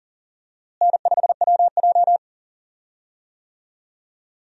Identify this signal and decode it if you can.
Morse code, heard as a keyed tone, about 37 wpm, 700 Hz. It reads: N5WJ
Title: USA Call Signs - 4 Characters - Repeated 2x - 40wpm